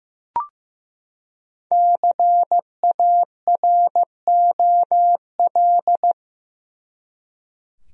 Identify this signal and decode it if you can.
Morse code, heard as a keyed tone, about 15 wpm, 700 Hz. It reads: CAROL
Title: USA Common Names - 15wpm